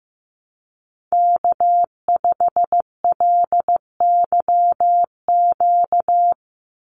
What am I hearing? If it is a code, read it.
Morse code, heard as a keyed tone, about 15 wpm, 700 Hz. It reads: K5LYQ
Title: USA Call Signs 15wpm